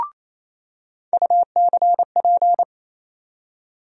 Morse code, heard as a keyed tone, about 28 wpm, 700 Hz. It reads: UCP
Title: USA Call Sign Postfixes - 3 Letters 28wpm